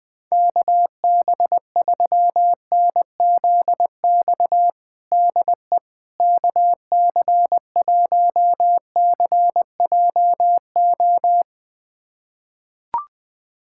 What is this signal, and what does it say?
Morse code, heard as a keyed tone, about 20 wpm, 700 Hz. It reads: KB3NZX DE KC1CJO
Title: USA Callsign DE Callsign 20wpm